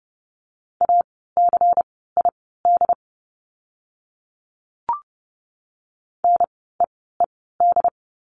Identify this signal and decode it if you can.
Morse code, heard as a keyed tone, about 30 wpm, spaced as for 17 wpm, 700 Hz. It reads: ACID NEED
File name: Lesson 16f - Letter c - Words 30wpm